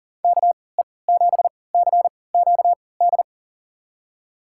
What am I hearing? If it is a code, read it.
Morse code, heard as a keyed tone, about 38 wpm, spaced as for 22 wpm, 700 Hz. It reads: KE7CQD
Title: USA Call Signs - 6 Characters - 3x Character Spacing - 40wpm